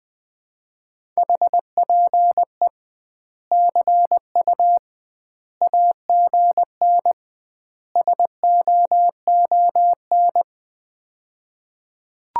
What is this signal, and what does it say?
Morse code, heard as a keyed tone, about 21 wpm, 700 Hz. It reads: HPE CU AGN SOON
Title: Common QSO Phrases - 2x Word-Spacing - 20wpm